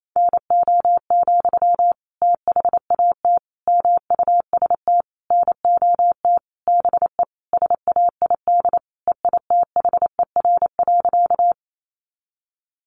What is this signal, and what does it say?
Morse code, heard as a keyed tone, about 28 wpm, 700 Hz. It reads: NO, THAT MUST NOT BE SAID EITHER.
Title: Sentences from Top 600 Words 28wpm